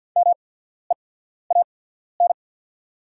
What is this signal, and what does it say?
Morse code, heard as a keyed tone, about 48 wpm, spaced as for 14 wpm, 700 Hz. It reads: MEAN
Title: 8x Character Spacing - Top 200 Words - 50wpm